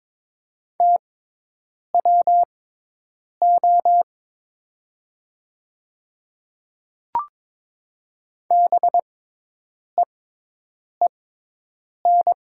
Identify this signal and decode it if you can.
Morse code, heard as a keyed tone, about 22 wpm, spaced as for 8 wpm, 700 Hz. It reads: TWO BEEN
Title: 6x Character Spacing - Top 300 Words - 22wpm